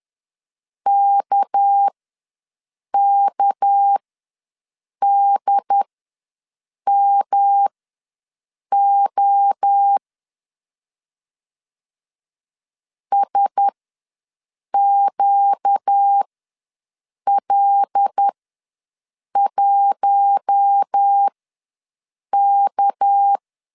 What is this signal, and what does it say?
Morse code, heard as a keyed tone, about 10 wpm, spaced as for 6 wpm, 800 Hz. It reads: KKDMO SQL1K
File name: LEZIONI / 6 PKX0 / 6 PK ws 10wpm 3745